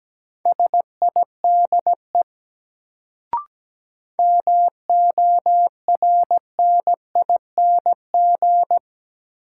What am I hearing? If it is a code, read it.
Morse code, heard as a keyed tone, about 17 wpm, 700 Hz. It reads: SIDE MORNING
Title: Sentences from Top 400 Words - New Word Drill - 17wpm